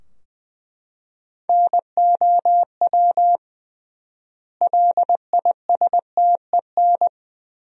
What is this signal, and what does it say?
Morse code, heard as a keyed tone, about 20 wpm, 700 Hz. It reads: NOW LISTEN
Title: 3x Word-Spacing - Sentences from Top 400 Words - 20wpm